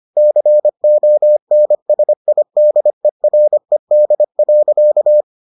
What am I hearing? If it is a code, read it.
Morse code, heard as a keyed tone, about 25 wpm, 600 Hz. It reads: CONSIDERED.